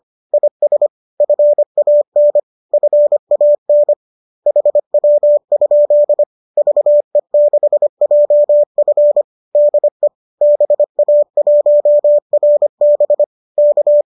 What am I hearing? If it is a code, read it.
Morse code, heard as a keyed tone, about 25 wpm, 590 Hz. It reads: IS FAN FAN HW? VE6JF DE BA1RB K
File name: qso203 25